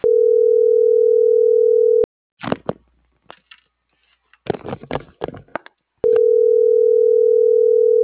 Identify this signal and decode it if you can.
Ringback tone (North America), 440 + 480 Hz.